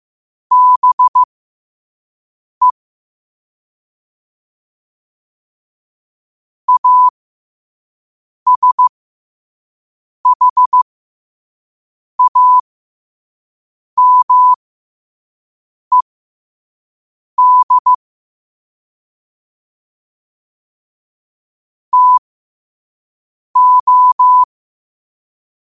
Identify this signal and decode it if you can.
Morse code, heard as a keyed tone, about 15 wpm, spaced as for 5 wpm, 1000 Hz. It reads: BE ASHAMED TO
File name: qotdfiles / 5wpm